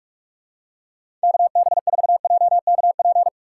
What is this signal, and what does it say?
Morse code, heard as a keyed tone, about 46 wpm, 700 Hz. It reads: KB4JKP
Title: USA Call Signs - 6 Characters - 2x Character Spacing - 45wpm